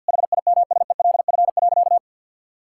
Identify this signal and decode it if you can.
Morse code, heard as a keyed tone, about 50 wpm, 700 Hz. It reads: HIMSELF.
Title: Sentences from Top 1600 Words 50wpm